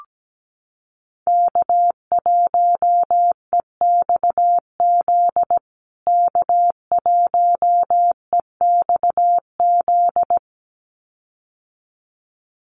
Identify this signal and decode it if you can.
Morse code, heard as a keyed tone, about 17 wpm, 700 Hz. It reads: K1EXZ K1EXZ
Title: USA Call Signs Repeated 2x 17wpm